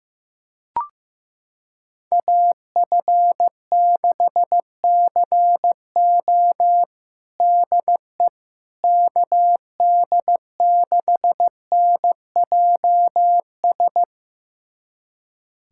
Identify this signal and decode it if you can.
Morse code, heard as a keyed tone, about 15 wpm, 700 Hz. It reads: AF6CO DE KD6NJS
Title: USA Callsign DE Callsign 15wpm